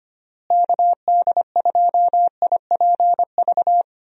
Morse code, heard as a keyed tone, about 25 wpm, 700 Hz. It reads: KD2IPV
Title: USA Call Signs - 6 Characters - Repeated 3x - 25wpm